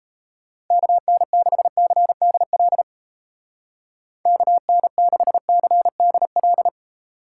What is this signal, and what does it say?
Morse code, heard as a keyed tone, about 31 wpm, 700 Hz. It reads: KN6CDL KN6CDL
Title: Speed Racing - USA 2x3 Call Signs 28wpm